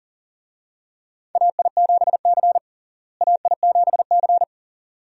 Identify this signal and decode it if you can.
Morse code, heard as a keyed tone, about 37 wpm, 700 Hz. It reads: AI7C AI7C
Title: USA Call Signs - 4 Characters - Repeated 2x - 40wpm